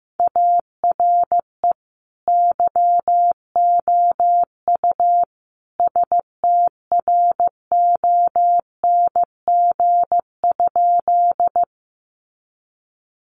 Morse code, heard as a keyed tone, about 15 wpm, 700 Hz. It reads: ARE YOU STRONG?
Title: Sentences from Top 500 Words 15wpm